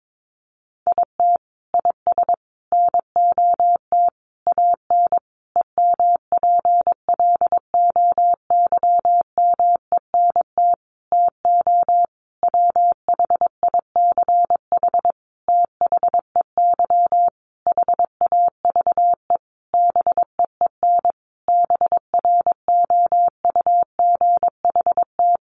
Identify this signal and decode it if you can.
Morse code, heard as a keyed tone, about 22 wpm, 700 Hz. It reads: IT IS NOT AN EMPLOYMENT TO WHICH THEY HAVE BEEN BROUGHT
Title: Sentences from Top 1500 Words 22wpm